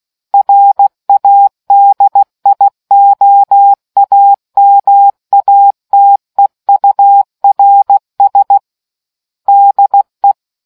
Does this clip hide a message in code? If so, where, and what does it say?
Yes: Morse code from 0.3 to 10.3 s, heard as a keyed tone, about 16 wpm, 800 Hz. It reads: RADIOAMATEURS DE